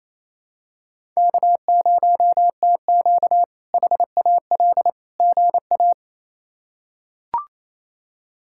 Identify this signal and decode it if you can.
Morse code, heard as a keyed tone, about 28 wpm, 700 Hz. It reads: K0TQ HAL GA
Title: K1USN SST QSO Practice - 28wpm